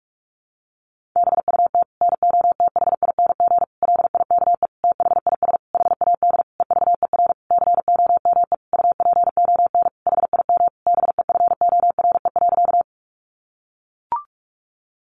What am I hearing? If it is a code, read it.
Morse code, heard as a keyed tone, about 44 wpm, 700 Hz. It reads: BUT NOTHING LIKE THIS HAD EVER COME UPON HIM BEFORE.
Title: Sentences from Top 1100 Words 45wpm